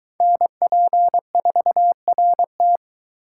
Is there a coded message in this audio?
Yes: Morse code from 0.2 to 2.8 s, heard as a keyed tone, about 23 wpm, 700 Hz. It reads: NP4RT